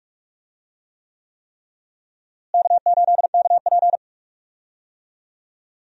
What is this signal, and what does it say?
Morse code, heard as a keyed tone, about 46 wpm, 700 Hz. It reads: K8KP
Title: USA Call Signs - 4 Characters - Repeated 3x - 45wpm